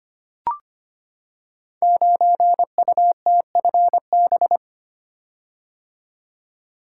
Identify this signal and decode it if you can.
Morse code, heard as a keyed tone, about 25 wpm, 700 Hz. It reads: 9UTFB
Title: Code Groups - 5 Letter:Numbers - 25wpm